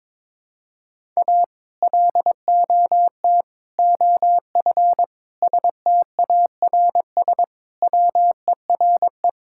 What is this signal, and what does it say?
Morse code, heard as a keyed tone, about 22 wpm, 700 Hz. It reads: A LOT OF STARS WERE
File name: practice / 22wpm / Sentences with 2-Word Phrases - 22wpm